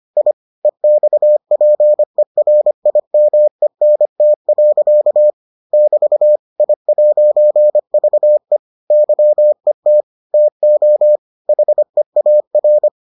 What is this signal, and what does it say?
Morse code, heard as a keyed tone, about 25 wpm, 600 Hz. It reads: I EXPERIMENT. <BT> I'VE YET TO HEAR